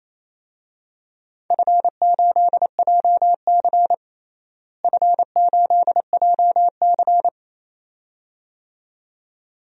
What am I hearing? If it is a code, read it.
Morse code, heard as a keyed tone, about 28 wpm, 700 Hz. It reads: F8JC F8JC
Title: DXCC-100 Call Signs - 4 Characters - Repeated 2x - 28wpm